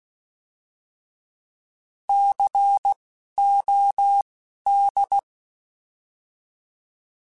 Morse code, heard as a keyed tone, about 16 wpm, spaced as for 12 wpm, 800 Hz. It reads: COD
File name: Alphabetical / Alphabetical mp3s / Cs @10